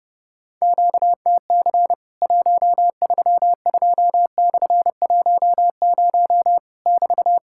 Morse code, heard as a keyed tone, about 30 wpm, 700 Hz. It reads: QTC 132/10 <BT>
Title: Worked All Europe DX Contest - QTC Exchange - 30wpm